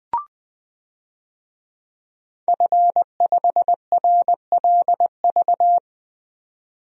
Morse code, heard as a keyed tone, about 20 wpm, 700 Hz. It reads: F5RLV
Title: DXCC-100 Call Signs - 5 Characters - 20wpm